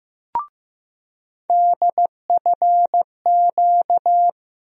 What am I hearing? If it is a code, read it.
Morse code, heard as a keyed tone, about 15 wpm, 700 Hz. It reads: DFQ
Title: USA Call Sign Postfixes - 3 Letters 15wpm